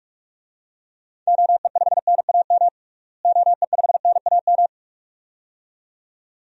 Morse code, heard as a keyed tone, about 46 wpm, 700 Hz. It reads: OE5NAM OE5NAM
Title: DXCC-100 Call Signs - 6 Characters - Repeated 2x - 45wpm